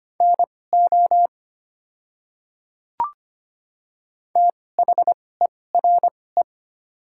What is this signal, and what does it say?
Morse code, heard as a keyed tone, about 25 wpm, spaced as for 18 wpm, 700 Hz. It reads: NO THERE
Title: 2x Character Spacing - Top 100 Words - 25wpm